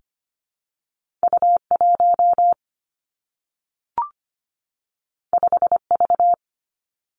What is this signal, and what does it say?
Morse code, heard as a keyed tone, about 25 wpm, 700 Hz. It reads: U 1 5 V
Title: Two Challenging Letter-Number Combinations 25wpm